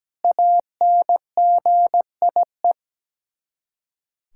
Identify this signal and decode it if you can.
Morse code, heard as a keyed tone, about 17 wpm, 700 Hz. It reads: ANGIE
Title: Practice QSO - Exchange 1 - Beginning - 17wpm